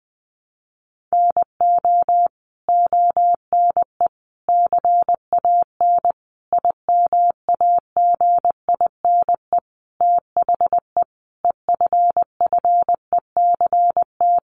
Morse code, heard as a keyed tone, about 20 wpm, 700 Hz. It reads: NO ONE CAN IMAGINE THE EFFECT
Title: Sentences from Top 1400 Words 20wpm